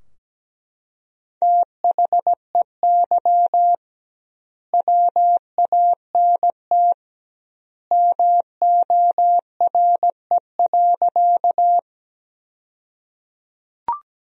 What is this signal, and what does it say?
Morse code, heard as a keyed tone, about 17 wpm, 700 Hz. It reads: THEY WANT MORE.
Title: 2x Word-Spacing - Sentences from Top 200 Words - 17wpm